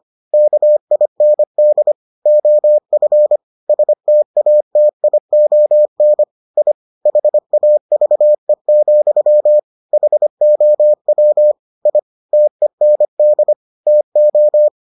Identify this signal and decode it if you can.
Morse code, heard as a keyed tone, about 25 wpm, 600 Hz. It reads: KIND OF STATION I HAVE, HOW I TEND TO